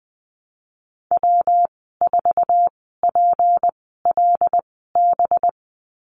Morse code, heard as a keyed tone, about 20 wpm, spaced as for 15 wpm, 700 Hz. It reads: W4PLB
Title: 207 Lesson 33g - Letter j - Call Signs 20wpm